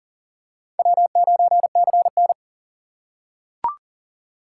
Morse code, heard as a keyed tone, about 40 wpm, 700 Hz. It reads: W9CN
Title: USA Call Signs - 2x Character Spacing - 40wpm